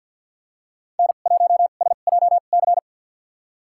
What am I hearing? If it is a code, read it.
Morse code, heard as a keyed tone, about 51 wpm, spaced as for 36 wpm, 700 Hz. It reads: N1SJC
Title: USA Call Signs - 5 Characters - 2x Character Spacing - 50wpm